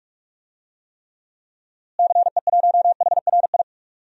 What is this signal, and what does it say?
Morse code, heard as a keyed tone, about 46 wpm, 700 Hz. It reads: CE1HRI
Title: DXCC-100 Call Signs - 6 Characters - Repeated 2x - 45wpm